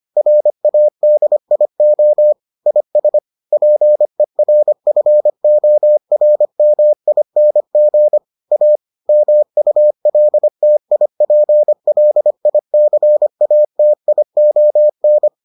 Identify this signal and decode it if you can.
Morse code, heard as a keyed tone, about 25 wpm, 600 Hz. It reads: RADIO IS PERFORMING A MULTIPLICATION